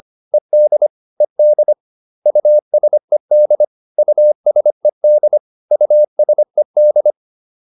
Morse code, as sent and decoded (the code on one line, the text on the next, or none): . -.. / . -.. / ..- ... -.. / ..- ... -.. / ..- ... -..
ED ED USED USED USED